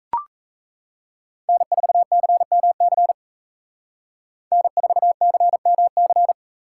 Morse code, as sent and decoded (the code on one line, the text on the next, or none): -. ....- -.-. -- -.-. / -. ....- -.-. -- -.-.
N4CMC N4CMC